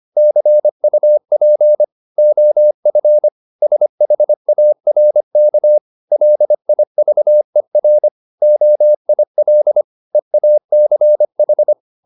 Morse code, heard as a keyed tone, about 25 wpm, 600 Hz. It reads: CUP OF SHARK LIVER OIL EACH
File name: the old man and the sea part 1 25wpm